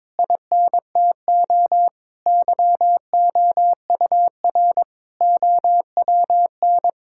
Morse code, heard as a keyed tone, about 22 wpm, 700 Hz. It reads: INTO YOUR OWN
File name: Sentences with 7-Word Phrases - 22wpm